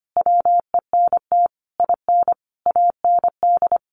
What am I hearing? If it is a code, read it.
Morse code, heard as a keyed tone, about 25 wpm, 700 Hz. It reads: WENT IN AND
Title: Sentences from Top 1400 Words 25wpm